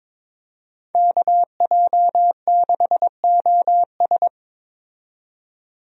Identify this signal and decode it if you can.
Morse code, heard as a keyed tone, about 22 wpm, 700 Hz. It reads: KJ6OS
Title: USA Call Signs - 5 Characters - 22wpm